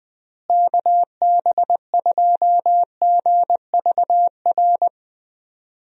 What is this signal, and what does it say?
Morse code, heard as a keyed tone, about 20 wpm, 700 Hz. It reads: KB2GVR
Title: USA Call Signs - 6 Characters - Repeated 3x - 20wpm